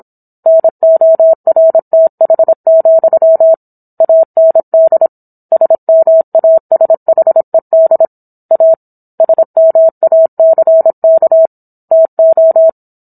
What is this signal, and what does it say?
Morse code, heard as a keyed tone, about 27 wpm, 650 Hz. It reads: NORTH, AND SMASHED A SMACK TO